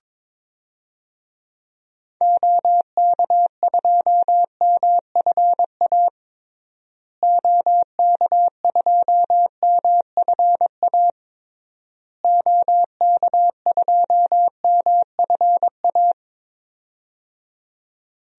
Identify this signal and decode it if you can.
Morse code, heard as a keyed tone, about 22 wpm, 700 Hz. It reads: OK2MFA OK2MFA OK2MFA
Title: DXCC-100 Call Signs - 6 Characters - Repeated 3x - 22wpm